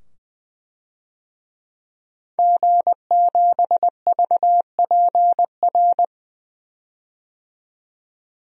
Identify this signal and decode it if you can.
Morse code, heard as a keyed tone, about 21 wpm, 700 Hz. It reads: G7VPR
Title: DXCC-100 Call Signs - 5 Characters - 20wpm